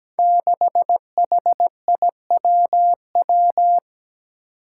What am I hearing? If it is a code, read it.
Morse code, heard as a keyed tone, about 17 wpm, 700 Hz. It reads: 6HIWW
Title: Five Challenging Letter-Number Combinations 17wpm